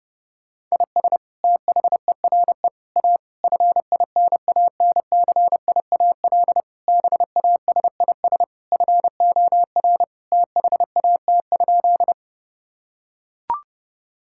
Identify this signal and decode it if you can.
Morse code, heard as a keyed tone, about 32 wpm, 700 Hz. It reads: IS THERE A FINANCIAL BASIS FOR THAT?
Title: Sentences from Top 1000 Words - Review Part 5 - 30wpm